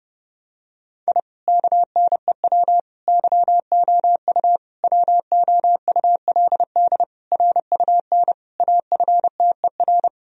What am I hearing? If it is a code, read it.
Morse code, heard as a keyed tone, about 32 wpm, 700 Hz. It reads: I KNEW YOU WOULD RUN AFTER
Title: Sentences from Top 1000 Words - Review Part 1 - 30wpm